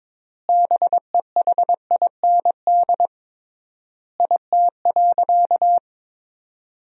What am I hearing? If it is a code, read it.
Morse code, heard as a keyed tone, about 22 wpm, 700 Hz. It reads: BEHIND IT.
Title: 3x Word-Spacing - Sentences from Top 500 Words - 22wpm